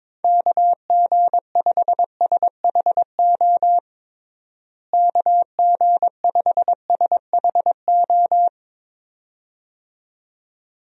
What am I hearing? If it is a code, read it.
Morse code, heard as a keyed tone, about 22 wpm, 700 Hz. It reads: KG5SHO KG5SHO
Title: USA Call Signs - 6 Characters - Repeated 2x - 22wpm